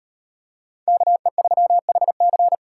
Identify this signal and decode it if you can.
Morse code, heard as a keyed tone, about 38 wpm, 700 Hz. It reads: KE3HC